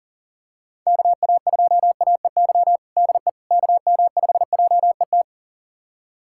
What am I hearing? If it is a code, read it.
Morse code, heard as a keyed tone, about 40 wpm, 700 Hz. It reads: KA2AEY DE KM5JET
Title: USA Callsign DE Callsign - 6 Characters - 40wpm